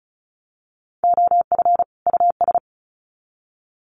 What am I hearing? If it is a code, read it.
Morse code, heard as a keyed tone, about 34 wpm, 700 Hz. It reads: OF US
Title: Sets of 2 Words 35wpm